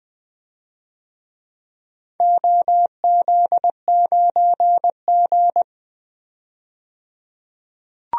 Morse code, heard as a keyed tone, about 21 wpm, 700 Hz. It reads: OZ9G
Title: DXCC-100 Call Signs - 4 Characters - 20wpm